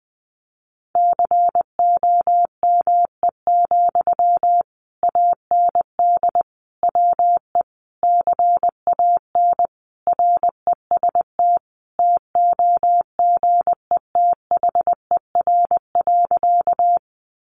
Morse code, heard as a keyed tone, about 20 wpm, 700 Hz. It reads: COME, AND WE CAN REST TOGETHER.
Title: Sentences from Top 500 Words 20wpm